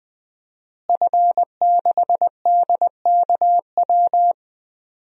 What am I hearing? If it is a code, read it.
Morse code, heard as a keyed tone, about 21 wpm, 700 Hz. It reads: F6DKW
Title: DXCC-100 Call Signs - 5 Characters - Repeated 3x - 20wpm